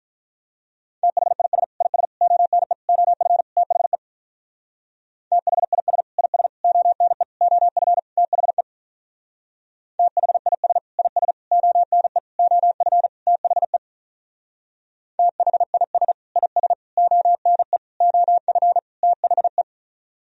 Morse code, heard as a keyed tone, about 43 wpm, 700 Hz. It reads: THIS IS ONE OF THE THIS IS ONE OF THE THIS IS ONE OF THE THIS IS ONE OF THE
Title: Speed-Racing - Sets of 5 Words 35wpm